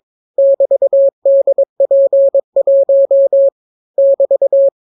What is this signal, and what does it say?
Morse code, heard as a keyed tone, about 22 wpm, 550 Hz. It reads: <BT>DP1 <BT>